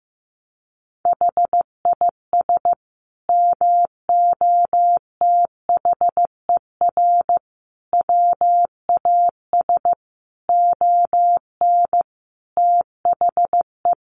Morse code, heard as a keyed tone, about 15 wpm, 700 Hz. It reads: HIS MOTHER WAS ON THE
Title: Sentences from Top 1200 Words 15wpm